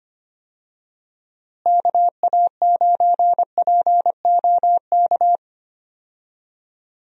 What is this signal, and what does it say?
Morse code, heard as a keyed tone, about 25 wpm, 700 Hz. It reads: KA9POK
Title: USA Call Signs - 6 Characters - Repeated 3x - 25wpm